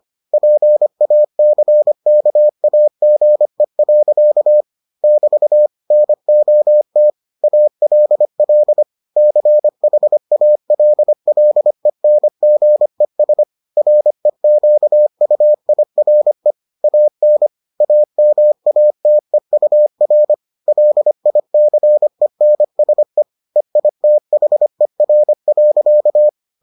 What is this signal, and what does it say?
Morse code, heard as a keyed tone, about 25 wpm, 600 Hz. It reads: PACKAGE. <BT> NOT ALL CHALLENGES REQUIRE AN AMATEUR LICENSE EITHER.